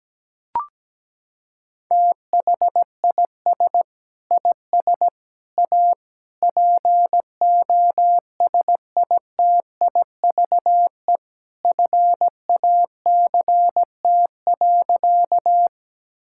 Morse code, heard as a keyed tone, about 17 wpm, 700 Hz. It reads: THIS IS A POSITIVE FACT.
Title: Sentences from Top 1000 Words - Review Part 5 - 17wpm